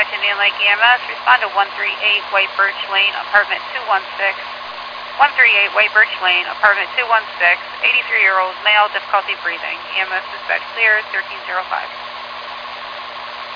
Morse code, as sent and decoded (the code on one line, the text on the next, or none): none